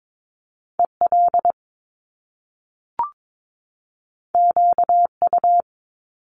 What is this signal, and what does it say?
Morse code, heard as a keyed tone, about 22 wpm, 700 Hz. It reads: EL QU
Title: Top Two Letter Combinations in Words 22wpm